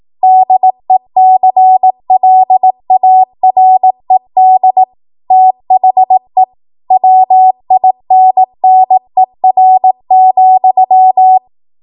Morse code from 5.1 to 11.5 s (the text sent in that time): THE WINNER,